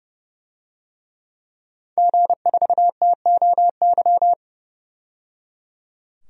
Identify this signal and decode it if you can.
Morse code, heard as a keyed tone, about 31 wpm, 700 Hz. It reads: G4TOY
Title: DXCC-100 Call Signs - 5 Characters - 30wpm